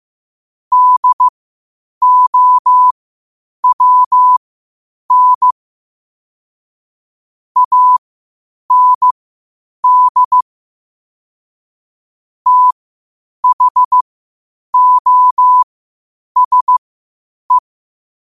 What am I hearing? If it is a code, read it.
Morse code, heard as a keyed tone, about 15 wpm, spaced as for 8 wpm, 1000 Hz. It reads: DOWN AND THOSE
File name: qotdfiles / 8wpm